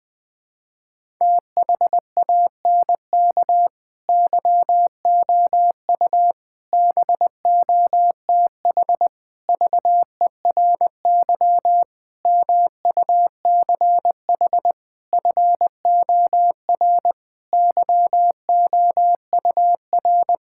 Morse code, heard as a keyed tone, about 20 wpm, 700 Hz. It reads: THANK YOU BOTH VERY MUCH FOR YOUR
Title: Sentences with 5-Word Phrases - 20wpm